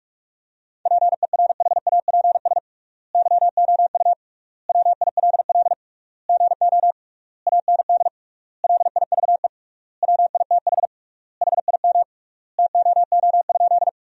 Morse code, heard as a keyed tone, about 43 wpm, 700 Hz. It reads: PERHAPS YOU WILL GO AND LIVE WITH HIM TOO?
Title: 3x Word-Spacing - Sentences from Top 500 Words - 45wpm